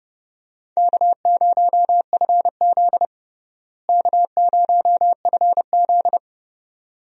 Morse code, heard as a keyed tone, about 30 wpm, 700 Hz. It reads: K0FZ K0FZ
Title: USA Call Signs - 4 Characters - Repeated 2x - 30wpm